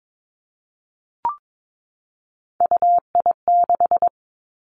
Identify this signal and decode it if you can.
Morse code, heard as a keyed tone, about 22 wpm, 700 Hz. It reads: UI6
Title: Three Challenging Letter-Number Combinations 22wpm